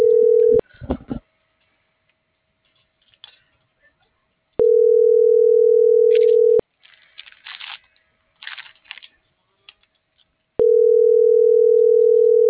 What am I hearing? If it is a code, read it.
Ringback tone (North America), 440 + 480 Hz.
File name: CENTRAL MAYAB / out lizbeth.mayab